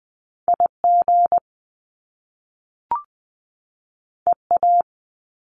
Morse code, heard as a keyed tone, about 20 wpm, 700 Hz. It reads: IG EA
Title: Top Two Letter Combinations in Words 20wpm